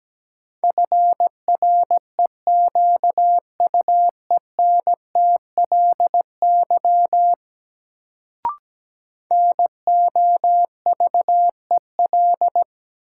Morse code, heard as a keyed tone, about 17 wpm, 700 Hz. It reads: FREQUENTLY NOVEL
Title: Sentences from Top 1300 Words - New Word Drill - 17wpm